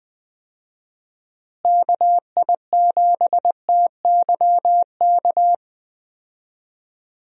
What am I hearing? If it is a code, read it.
Morse code, heard as a keyed tone, about 20 wpm, 700 Hz. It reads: KI7TYK
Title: USA Call Signs - 6 Characters - Repeated 2x - 20wpm